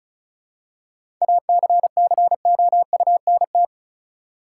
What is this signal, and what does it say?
Morse code, heard as a keyed tone, about 34 wpm, 700 Hz. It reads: ACCOUNT
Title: Seven Letter Words - 35wpm